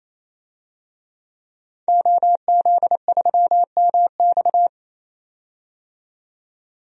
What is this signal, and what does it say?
Morse code, heard as a keyed tone, about 28 wpm, 700 Hz. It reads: OZ3MX
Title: DXCC-100 Call Signs - 5 Characters - 28wpm